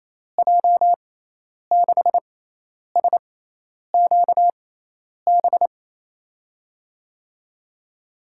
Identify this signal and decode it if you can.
Morse code, heard as a keyed tone, about 28 wpm, spaced as for 10 wpm, 700 Hz. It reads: J6SQB
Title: USA Call Signs - 6 Characters - 6x Character Spacing - 28wpm